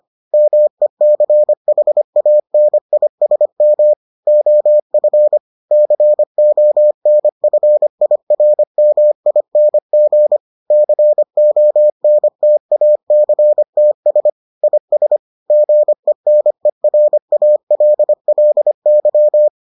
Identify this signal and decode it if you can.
Morse code, heard as a keyed tone, about 25 wpm, 600 Hz. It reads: MECHANISM OF CONFIRMING CONTACTS IS GENERALLY